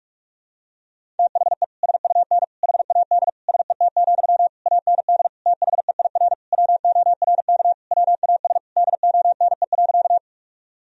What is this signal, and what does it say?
Morse code, heard as a keyed tone, about 46 wpm, 700 Hz. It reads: THE SUN HAD SET, AND THEIR WORK WAS DONE.
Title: Sentences from Top 1000 Words - Review Part 1 - 45wpm